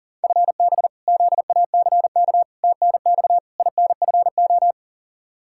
Morse code, heard as a keyed tone, about 40 wpm, 700 Hz. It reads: FB ZACK TNX INFO
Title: Practice QSO - Exchange 3 - More Info - 40wpm